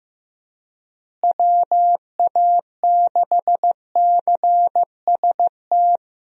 Morse code, heard as a keyed tone, about 15 wpm, 700 Hz. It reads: WA6CST
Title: USA Call Signs - 6 Characters - Repeated 3x - 15wpm